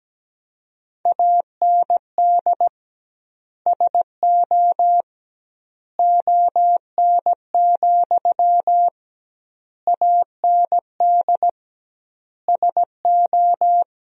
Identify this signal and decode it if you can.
Morse code, heard as a keyed tone, about 17 wpm, 700 Hz. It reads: AND SO ON, AND SO
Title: 2x Word-Spacing - Sentences from Top 100 Words - 17wpm